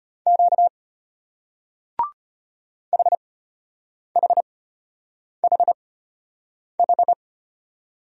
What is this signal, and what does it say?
Morse code, heard as a keyed tone, about 34 wpm, spaced as for 8 wpm, 700 Hz. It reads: Q HHHH